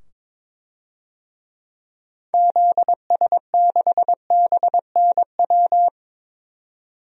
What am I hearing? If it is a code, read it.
Morse code, heard as a keyed tone, about 22 wpm, 700 Hz. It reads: ZS6BNW
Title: DXCC-100 Call Signs - 6 Characters - 22wpm